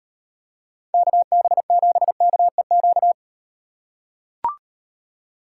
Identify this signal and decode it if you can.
Morse code, heard as a keyed tone, about 36 wpm, 700 Hz. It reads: KB7KEQ